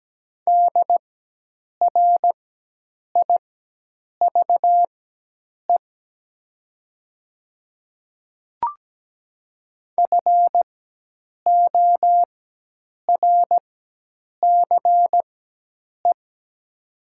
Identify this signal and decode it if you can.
Morse code, heard as a keyed tone, about 17 wpm, spaced as for 8 wpm, 700 Hz. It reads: DRIVE FORCE
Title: 4x Character Spacing - Top 500 Words - 17wpm